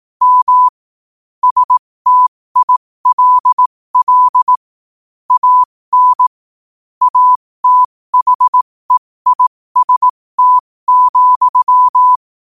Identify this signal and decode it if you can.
Morse code, heard as a keyed tone, about 18 wpm, spaced as for 15 wpm, 1000 Hz. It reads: M STILL AN ATHEIST,
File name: qotdfiles / 15wpm